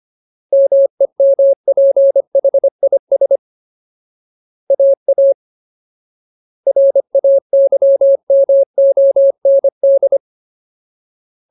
Morse code, heard as a keyed tone, about 25 wpm, 550 Hz. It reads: MEMPHIS AA RAYMOND
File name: Ultimate Word List / 25WPM Ultimate